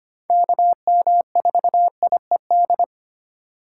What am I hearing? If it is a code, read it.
Morse code, heard as a keyed tone, about 25 wpm, 700 Hz. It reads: KM4IED
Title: USA Call Signs - 6 Characters - 4x Character Spacing - 25wpm